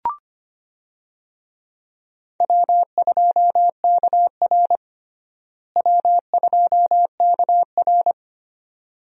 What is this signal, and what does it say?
Morse code, heard as a keyed tone, about 25 wpm, 700 Hz. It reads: W2KR W2KR
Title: USA Call Signs - 4 Characters - Repeated 2x - 25wpm